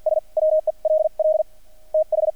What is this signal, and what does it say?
Morse code, heard as a keyed tone, about 50 wpm, spaced as for 41 wpm, 620 Hz. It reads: S WELL TH
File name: archived / CRSnet